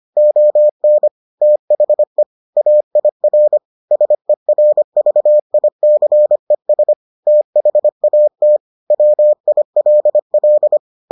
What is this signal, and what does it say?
Morse code, heard as a keyed tone, about 25 wpm, 600 Hz. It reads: ON THE AIR SERVICES THAT WILL